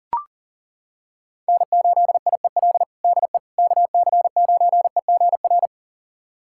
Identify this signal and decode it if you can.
Morse code, heard as a keyed tone, about 40 wpm, 700 Hz. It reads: N8IEL DE KC9EGR
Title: USA Callsign DE Callsign 40wpm